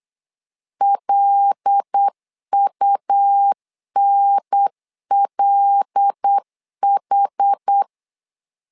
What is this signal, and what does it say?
Morse code, heard as a keyed tone, about 8 wpm, 800 Hz. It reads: LUNLH
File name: LEZIONI / 5 LBVC / 5 L 8wpm 2831